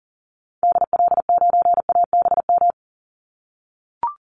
Morse code, heard as a keyed tone, about 40 wpm, 700 Hz. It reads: DL9ABM
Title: Call Signs - 40wpm